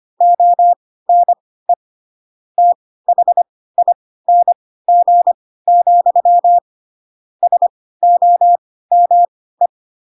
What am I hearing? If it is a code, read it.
Morse code, heard as a keyed tone, about 25 wpm, spaced as for 16 wpm, 700 Hz. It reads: ONE THING, SOME
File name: Pinocchio / Pinocchio ch3 10 25 ik1hge